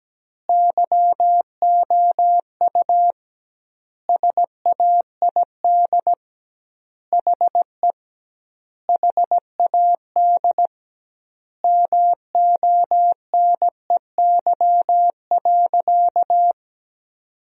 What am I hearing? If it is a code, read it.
Morse code, heard as a keyed tone, about 17 wpm, 700 Hz. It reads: YOU SAID HE HAD MONEY.
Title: 2x Word-Spacing - Sentences from Top 500 Words - 17wpm